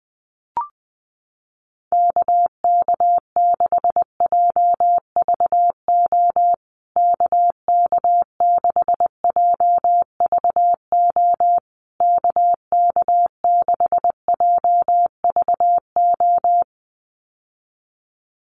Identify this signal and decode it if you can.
Morse code, heard as a keyed tone, about 20 wpm, 700 Hz. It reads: KK6JVO KK6JVO KK6JVO
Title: USA Call Signs Repeated 3x 20wpm